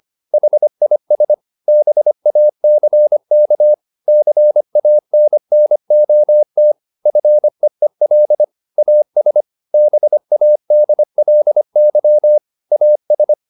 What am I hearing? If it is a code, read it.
Morse code, heard as a keyed tone, about 25 wpm, 600 Hz. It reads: HIS BACK CANNOT FEEL AS BADLY AS
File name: the old man and the sea part 1 25wpm